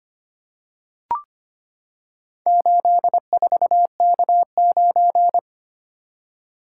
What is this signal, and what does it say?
Morse code, heard as a keyed tone, about 25 wpm, 700 Hz. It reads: 84K9
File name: practice / 25wpm / Four Letter-Number Combinations - 25wpm